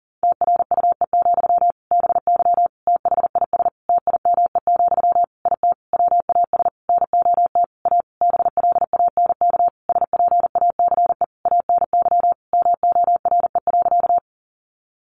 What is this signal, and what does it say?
Morse code, heard as a keyed tone, about 40 wpm, 700 Hz. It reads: TRUE, BY THIS TIME, IT WAS NOT A BLANK SPACE ANY MORE.
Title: Sentences from Top 1500 Words 40wpm